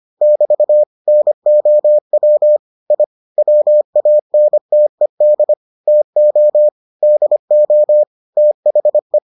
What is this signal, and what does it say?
Morse code, heard as a keyed tone, about 25 wpm, 600 Hz. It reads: <BT> NOW I WANTED TO DO THE